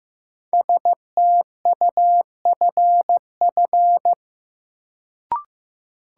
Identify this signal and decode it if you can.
Morse code, heard as a keyed tone, about 15 wpm, 700 Hz. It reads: STUFF
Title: Sentences from Top 1400 Words - New Word Drill Alt - 15wpm